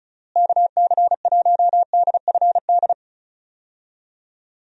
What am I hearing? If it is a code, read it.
Morse code, heard as a keyed tone, about 36 wpm, 700 Hz. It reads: KC1DFD
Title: USA Call Signs - 6 Characters - Repeated 2x - 35wpm